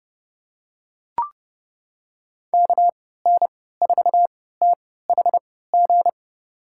Morse code, heard as a keyed tone, about 30 wpm, spaced as for 17 wpm, 700 Hz. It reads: KN4THG
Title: USA Call Signs - 6 Characters - 3x Character Spacing - 30wpm